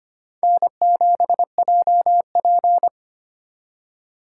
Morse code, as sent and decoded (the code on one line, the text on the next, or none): -. --... .--- .--.
N7JP